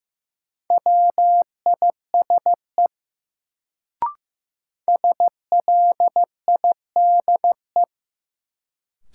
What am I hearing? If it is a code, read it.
Morse code, heard as a keyed tone, about 15 wpm, 700 Hz. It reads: WISE SLIDE
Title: Sentences from Top 1600 Words - New Word Drill - 15wpm